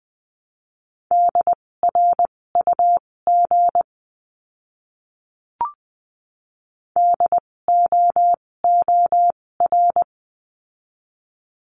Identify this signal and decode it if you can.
Morse code, heard as a keyed tone, about 20 wpm, spaced as for 16 wpm, 700 Hz. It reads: DRUG DOOR
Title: 256 Lesson 42a - Words 12-20wpm